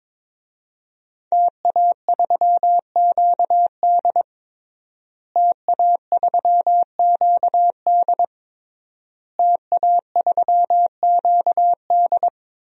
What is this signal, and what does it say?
Morse code, heard as a keyed tone, about 22 wpm, 700 Hz. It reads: TA3QD TA3QD TA3QD
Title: DXCC-100 Call Signs - 5 Characters - Repeated 3x - 22wpm